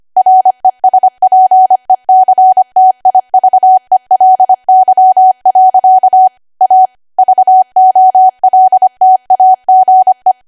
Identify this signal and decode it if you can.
Morse code, heard as a keyed tone, about 25 wpm, 750 Hz. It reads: RESPECTIVELY. A VOLTAGE
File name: Archive / 25 WPM / 25WPM